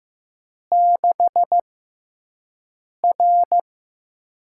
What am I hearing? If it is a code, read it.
Morse code, heard as a keyed tone, about 15 wpm, 700 Hz. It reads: 6 R
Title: USA Call Signs - 5 Characters - 6x Character Spacing - 15wpm